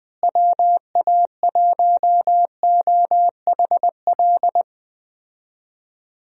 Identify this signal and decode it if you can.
Morse code, heard as a keyed tone, about 21 wpm, 700 Hz. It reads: WA1OHL
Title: USA Call Signs - 6 Characters - Repeated 2x - 20wpm